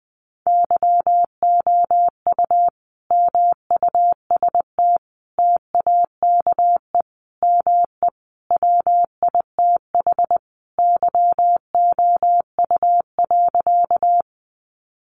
Morse code, heard as a keyed tone, about 20 wpm, 700 Hz. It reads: YOU MUST TAKE ME WITH YOU.
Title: Sentences from Top 200 Words 20wpm